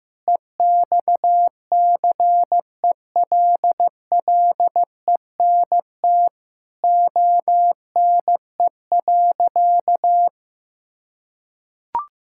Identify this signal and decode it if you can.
Morse code, heard as a keyed tone, about 15 wpm, 700 Hz. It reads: EXCELLENT ONE.
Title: Sentences from Top 1000 Words - Review Part 6 - 15wpm